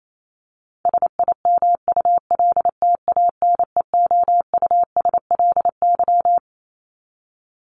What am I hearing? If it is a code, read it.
Morse code, heard as a keyed tone, about 28 wpm, 700 Hz. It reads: SIMULTANEOUSLY
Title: LY Suffix - Top 100 Words with Base Words 28wpm